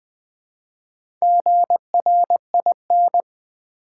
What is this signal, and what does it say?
Morse code, heard as a keyed tone, about 20 wpm, 700 Hz. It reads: GRIN
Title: Build A Word - 1 Letter At A Time - 4-Letter Words - 20wpm